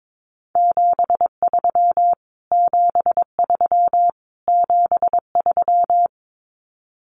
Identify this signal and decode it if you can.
Morse code, heard as a keyed tone, about 22 wpm, 700 Hz. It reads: 73 73 73
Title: Abbreviations Repeated 3x - 22wpm